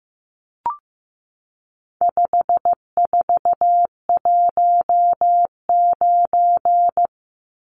Morse code, heard as a keyed tone, about 15 wpm, 700 Hz. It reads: 5419
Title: Numbers - 4-Digit 15wpm